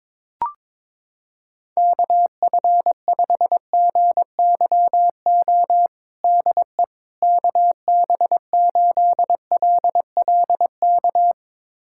Morse code, heard as a keyed tone, about 22 wpm, 700 Hz. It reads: KF5GYO DE KB8LLK
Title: USA Callsign DE Callsign 22wpm